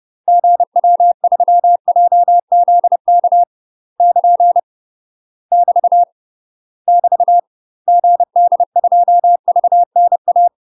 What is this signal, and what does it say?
Morse code, heard as a keyed tone, about 30 wpm, 700 Hz. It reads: GW3JZK <KN> <BT> <BT> GD2VNA
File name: UK MASTER QSOs TRAINING / 30WPM / UK MASTER QSOs 0026